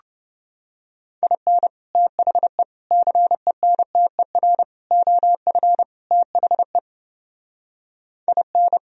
Morse code, heard as a keyed tone, about 30 wpm, 700 Hz. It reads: IN THE CENTER OF THE IN